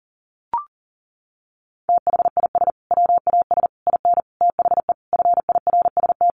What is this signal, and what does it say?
Morse code, heard as a keyed tone, about 37 wpm, 700 Hz. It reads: THIS WAS IN THE FIRST
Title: Sentences from Top 1300 Words 40wpm